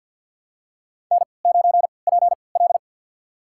Morse code, heard as a keyed tone, about 48 wpm, spaced as for 28 wpm, 700 Hz. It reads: N9PL
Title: USA Call Signs - 4 Characters - 3x Character Spacing - 50wpm